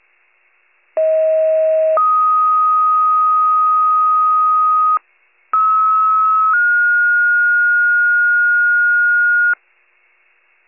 Two-tone page, heard as a one-second tone, then a three-second tone, about 640 then 1250 Hz, about 1290 then 1430 Hz. It is heard from 1.0 to 9.5 s.
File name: pager / 94 RESCUE DUTY / wcalert